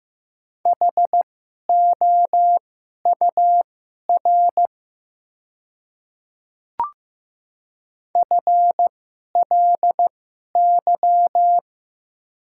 Morse code, heard as a keyed tone, about 15 wpm, spaced as for 11 wpm, 700 Hz. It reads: HOUR FLY
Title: 2x Character Spacing - Top 500 Words - 15wpm